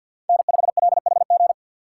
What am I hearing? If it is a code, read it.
Morse code, heard as a keyed tone, about 54 wpm, 700 Hz. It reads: N5LHG
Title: USA Call Signs - 4x Character Spacing - 50wpm